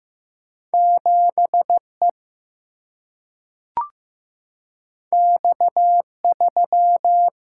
Morse code, heard as a keyed tone, about 15 wpm, 700 Hz. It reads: 7 E X 3
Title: Single-Letter Single-Number 15wpm